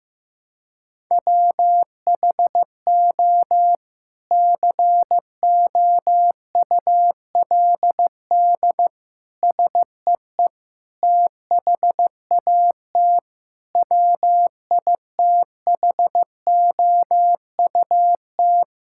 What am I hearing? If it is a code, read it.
Morse code, heard as a keyed tone, about 16 wpm, 700 Hz. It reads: WHO COULD SEE THAT WITHOUT
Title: Sentences from Top 1600 Words 15wpm